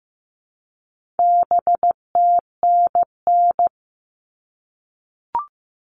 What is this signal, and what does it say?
Morse code, heard as a keyed tone, about 15 wpm, 700 Hz. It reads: BTNN
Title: All Cut Numbers - 4-Digit 15wpm